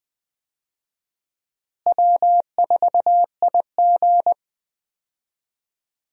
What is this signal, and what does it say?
Morse code, heard as a keyed tone, about 20 wpm, 700 Hz. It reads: W4IG
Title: USA Call Signs - 4 Characters - Repeated 3x - 20wpm